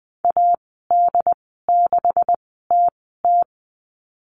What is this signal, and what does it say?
Morse code, heard as a keyed tone, about 20 wpm, spaced as for 15 wpm, 700 Hz. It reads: AD6TT
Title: 193 Lesson 31g - Letter v - Call Signs 20wpm